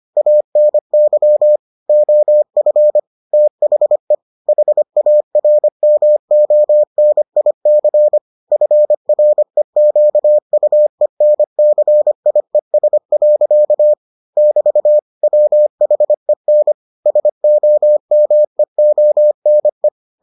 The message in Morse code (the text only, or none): ANY OF THE HARMONIC FREQUENCIES. <BT> WHEN SOMEONE